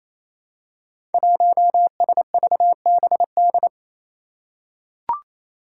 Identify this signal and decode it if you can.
Morse code, heard as a keyed tone, about 27 wpm, 700 Hz. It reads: 1SVBD
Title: Five Challenging Letter-Number Combinations 28wpm